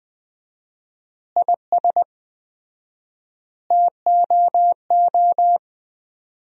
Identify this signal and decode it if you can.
Morse code, heard as a keyed tone, about 20 wpm, 700 Hz. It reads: IS TOO
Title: 4x Word-Spacing - Sentences from Top 500 Words - 20wpm